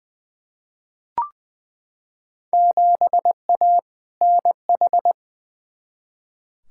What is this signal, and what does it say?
Morse code, heard as a keyed tone, about 21 wpm, 700 Hz. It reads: 7A NH
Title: Field Day Exchanges - 20wpm